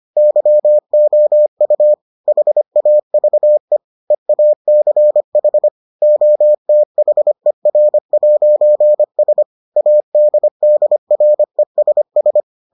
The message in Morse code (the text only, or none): YOU HAVE EACH OTHER'S ADDRESS